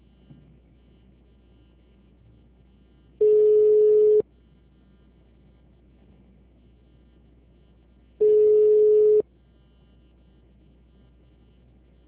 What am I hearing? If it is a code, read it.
Ringback tone (Europe, 425 Hz).
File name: out- -2007- - - .102